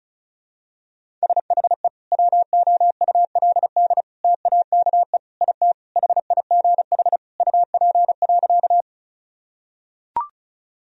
Morse code, heard as a keyed tone, about 34 wpm, 700 Hz. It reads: SHE WOULD TAKE IT HIGH UP.
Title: Sentences from Top 1000 Words - Review Part 1 - 35wpm